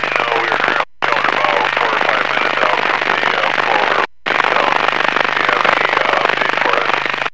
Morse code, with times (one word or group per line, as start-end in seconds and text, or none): none